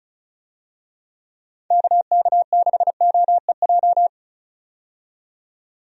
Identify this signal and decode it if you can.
Morse code, heard as a keyed tone, about 34 wpm, 700 Hz. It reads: KK6OEJ
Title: USA Call Signs - 6 Characters - 35wpm